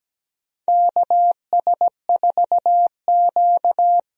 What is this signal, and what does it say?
Morse code, heard as a keyed tone, about 17 wpm, 700 Hz. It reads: KS4Q
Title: USA Call Signs - 4 Characters - 17wpm